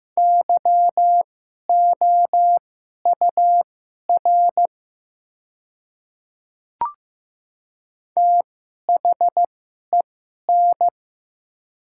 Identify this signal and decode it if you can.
Morse code, heard as a keyed tone, about 15 wpm, spaced as for 11 wpm, 700 Hz. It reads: YOUR THEN
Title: 2x Character Spacing - Top 100 Words - 15wpm